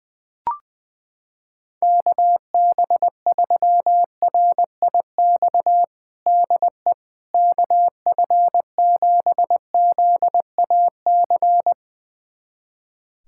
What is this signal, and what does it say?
Morse code, heard as a keyed tone, about 21 wpm, 700 Hz. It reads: KB3RIX DE KF7ZAC
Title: USA Callsign DE Callsign - 6 Characters - 20wpm